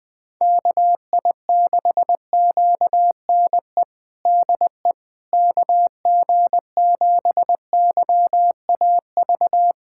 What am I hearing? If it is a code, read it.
Morse code, heard as a keyed tone, about 20 wpm, 700 Hz. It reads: KI6QNE DE KG7YAV
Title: USA Callsign DE Callsign 20wpm